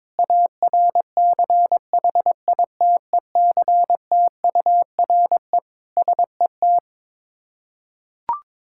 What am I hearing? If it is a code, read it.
Morse code, heard as a keyed tone, about 22 wpm, 690 Hz. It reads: ARCHITECTURE SET
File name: Difficult Words with Hints - 22wpm